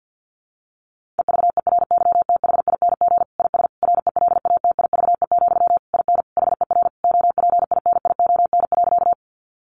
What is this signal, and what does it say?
Morse code, heard as a keyed tone, about 50 wpm, 700 Hz. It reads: EVERYTHING IS RELATIVE, IN HER OPINION.
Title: Sentences from Top 1400 Words 50wpm